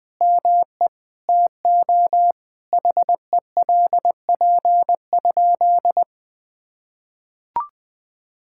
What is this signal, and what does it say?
Morse code, heard as a keyed tone, about 20 wpm, 700 Hz. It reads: ME TO HELP?
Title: Sentences with 5-Word Phrases - 20wpm